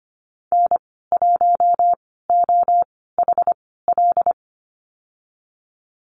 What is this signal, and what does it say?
Morse code, heard as a keyed tone, about 24 wpm, spaced as for 16 wpm, 700 Hz. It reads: N1OHL
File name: Lesson 15g - Number 5 - Call Signs 25wpm